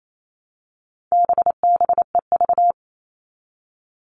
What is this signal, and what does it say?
Morse code, heard as a keyed tone, about 28 wpm, 700 Hz. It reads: BBEV
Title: All Cut Numbers - 4-Digit 28wpm